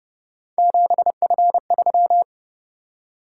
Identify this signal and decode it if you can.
Morse code, heard as a keyed tone, about 31 wpm, 700 Hz. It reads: 7F3